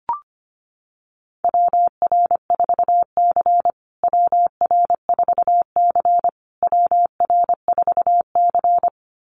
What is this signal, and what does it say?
Morse code, heard as a keyed tone, about 25 wpm, 700 Hz. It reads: WR4C WR4C WR4C
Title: USA Call Signs Repeated 3x 25wpm